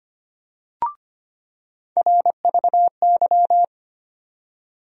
Morse code, heard as a keyed tone, about 25 wpm, 700 Hz. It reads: RVY
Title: USA Call Sign Postfixes - 3 Letters 25wpm